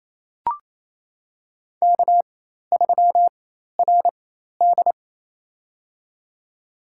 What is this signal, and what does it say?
Morse code, heard as a keyed tone, about 27 wpm, spaced as for 13 wpm, 700 Hz. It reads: K3RD
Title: USA Call Signs - 4x Character Spacing - 28wpm